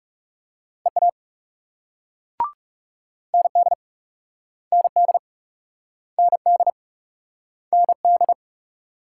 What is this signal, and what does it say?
Morse code, heard as a keyed tone, about 42 wpm, 700 Hz. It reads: EA ND ND ND ND